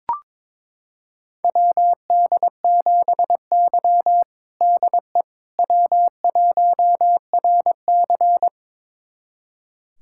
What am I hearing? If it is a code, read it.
Morse code, heard as a keyed tone, about 22 wpm, 700 Hz. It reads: WD7Y DE W1RC
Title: USA Callsign DE Callsign - 4 Characters - 22wpm